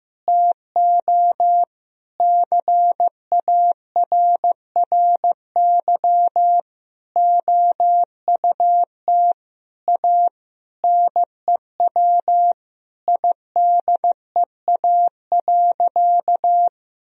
Morse code, heard as a keyed tone, about 16 wpm, 700 Hz. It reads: TO CARRY OUT A NEW IDEA.
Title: Sentences from Top 1000 Words - Review Part 1 - 15wpm